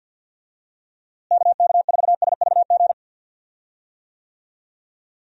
Morse code, heard as a keyed tone, about 52 wpm, 700 Hz. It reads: KK4SVG
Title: USA Call Signs - 6 Characters - Repeated 3x - 50wpm